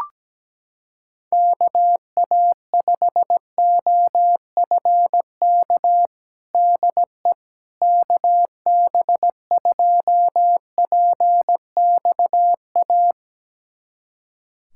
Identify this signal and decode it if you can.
Morse code, heard as a keyed tone, about 17 wpm, 700 Hz. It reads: KA5OFK DE KB2PXA
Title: USA Callsign DE Callsign - 6 Characters - 17wpm